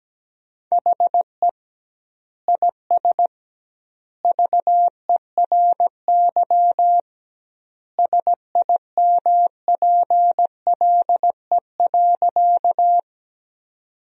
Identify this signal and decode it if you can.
Morse code, heard as a keyed tone, about 17 wpm, 700 Hz. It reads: HE IS VERY SIMPLE.
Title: 2x Word-Spacing - Sentences from Top 400 Words - 17wpm